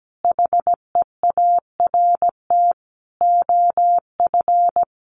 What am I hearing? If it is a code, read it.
Morse code, heard as a keyed tone, about 17 wpm, 700 Hz. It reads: HEART OF
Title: Sentences from Top 1200 Words 17wpm